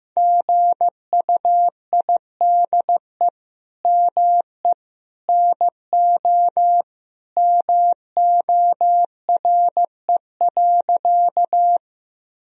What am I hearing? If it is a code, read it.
Morse code, heard as a keyed tone, about 15 wpm, 700 Hz. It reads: GUIDE ME NO MORE.
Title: Sentences from Top 1000 Words - Review Part 3 - 15wpm